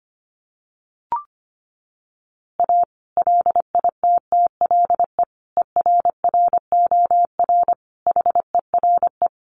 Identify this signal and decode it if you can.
Morse code, heard as a keyed tone, about 25 wpm, 700 Hz. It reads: A LITTLE ERROR HERE
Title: A Sound Of Thunder 25wpm